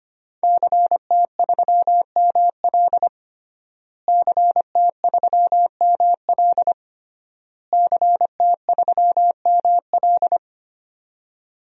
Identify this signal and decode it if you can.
Morse code, heard as a keyed tone, about 25 wpm, 700 Hz. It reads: CT3ML CT3ML CT3ML
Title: DXCC-100 Call Signs - 5 Characters - Repeated 3x - 25wpm